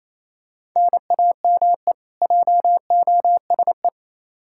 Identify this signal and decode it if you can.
Morse code, heard as a keyed tone, about 28 wpm, 700 Hz. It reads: NAME JOSE
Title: Practice QSO - Exchange 1 - Beginning - 28wpm